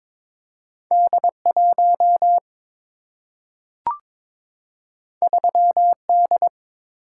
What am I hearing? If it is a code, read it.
Morse code, heard as a keyed tone, about 22 wpm, 700 Hz. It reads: D 1 3 D
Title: Single-Letter Single-Number 22wpm